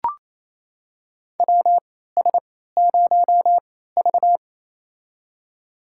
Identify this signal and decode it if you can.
Morse code, heard as a keyed tone, about 27 wpm, spaced as for 16 wpm, 700 Hz. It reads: WS0V
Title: USA Call Signs - 3x Character Spacing - 28wpm